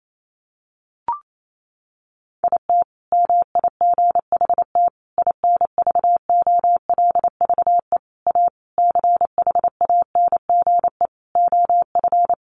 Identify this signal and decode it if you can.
Morse code, heard as a keyed tone, about 28 wpm, 700 Hz. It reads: IT MIGHT INVOLVE A CHANGE OF
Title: Sentences from Top 1200 Words 28wpm